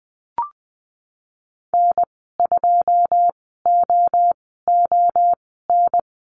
Morse code, heard as a keyed tone, about 20 wpm, spaced as for 15 wpm, 700 Hz. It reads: N2OON
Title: 122 Lesson 20g - Number 3 - Call Signs 20wpm